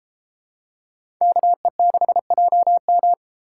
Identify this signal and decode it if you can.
Morse code, heard as a keyed tone, about 33 wpm, 700 Hz. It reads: KE6JM